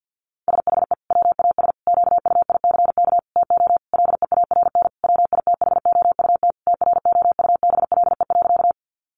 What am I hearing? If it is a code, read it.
Morse code, heard as a keyed tone, about 50 wpm, 700 Hz. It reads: SHE WAS QUICK TO LEARN WITHOUT TROUBLE.
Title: Sentences from Top 1200 Words 50wpm